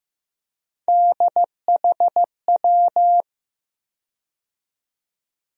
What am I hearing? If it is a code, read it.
Morse code, heard as a keyed tone, about 15 wpm, 700 Hz. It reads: DHW
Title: USA Call Sign Postfixes - 3 Letters 15wpm